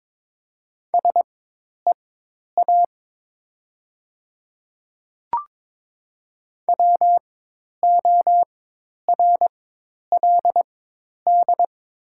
Morse code, heard as a keyed tone, about 22 wpm, spaced as for 10 wpm, 700 Hz. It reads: SEA WORLD
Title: 4x Character Spacing - Top 300 Words - 22wpm